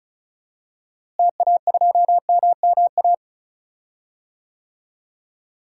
Morse code, heard as a keyed tone, about 36 wpm, 700 Hz. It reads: TA2MMA
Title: DXCC-100 Call Signs - 6 Characters - Repeated 2x - 35wpm